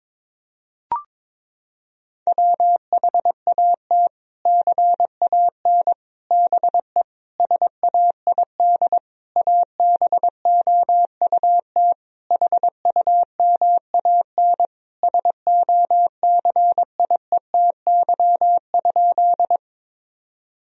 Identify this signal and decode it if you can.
Morse code, heard as a keyed tone, about 22 wpm, 700 Hz. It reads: WHAT CAN BE SAID ABOUT HUMAN SOCIETY?
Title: Sentences from Top 1000 Words - Review Part 4 - 22wpm